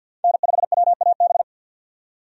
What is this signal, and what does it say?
Morse code, heard as a keyed tone, about 53 wpm, 700 Hz. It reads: N5WAB
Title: USA Call Signs - 5 Characters - 50wpm